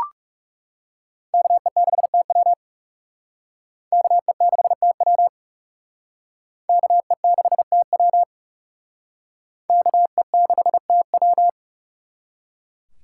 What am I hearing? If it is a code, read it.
Morse code, heard as a keyed tone, about 38 wpm, 700 Hz. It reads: KE6TW KE6TW KE6TW KE6TW
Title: Speed Racing - USA 2x2 Call Signs 30wpm